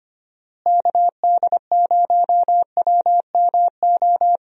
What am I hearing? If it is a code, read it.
Morse code, heard as a keyed tone, about 25 wpm, 700 Hz. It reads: KD0WMO
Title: USA Call Signs - 6 Characters - Repeated 3x - 25wpm